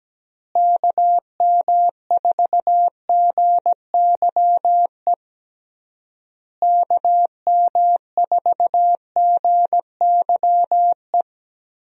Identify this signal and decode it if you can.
Morse code, heard as a keyed tone, about 17 wpm, 700 Hz. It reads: KM4GYE KM4GYE
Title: USA Call Signs - 6 Characters - Repeated 3x - 17wpm